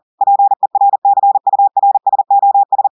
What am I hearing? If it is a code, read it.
Morse code, heard as a keyed tone, about 40 wpm, 800 Hz. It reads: PERCURSOS
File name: audios / portugais / portugais texte en clair 1 vitesse 40 mots minutes